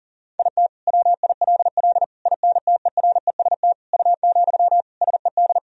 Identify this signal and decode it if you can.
Morse code, heard as a keyed tone, about 40 wpm, 700 Hz. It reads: IT WILL INTEREST U, SED
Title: Sentences from Top 400 Words and Abbreviations 40wpm